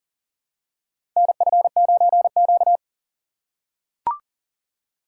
Morse code, heard as a keyed tone, about 40 wpm, 700 Hz. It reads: NF9Q
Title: USA Call Signs - 4 Characters - 6x Character Spacing - 40wpm